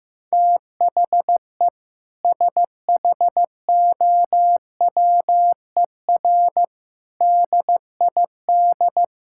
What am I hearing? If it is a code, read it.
Morse code, heard as a keyed tone, about 15 wpm, 700 Hz. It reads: THE SHOWER DID
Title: Sentences from Top 1600 Words 15wpm